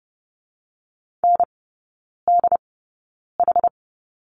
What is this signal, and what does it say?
Morse code, heard as a keyed tone, about 30 wpm, spaced as for 9 wpm, 700 Hz. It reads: NDH
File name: Lesson 18e - Letter m - Three Characters 30wpm